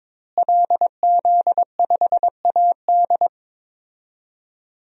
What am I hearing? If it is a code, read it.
Morse code, heard as a keyed tone, about 22 wpm, 700 Hz. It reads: LZ5AD
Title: DXCC-100 Call Signs - 5 Characters - Repeated 2x - 22wpm